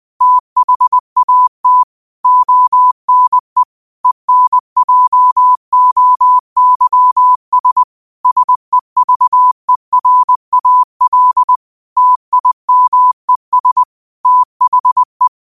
Morse code, heard as a keyed tone, about 20 wpm, 1000 Hz. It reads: THAT ONE ENJOYS SEVERAL TIMES THE